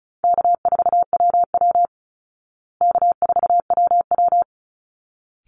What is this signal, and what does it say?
Morse code, heard as a keyed tone, about 34 wpm, 700 Hz. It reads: K4WW K4WW
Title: Easy Call Signs Repeated 2x - 35wpm